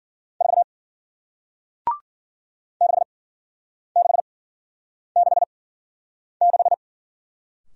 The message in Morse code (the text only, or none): 4 6666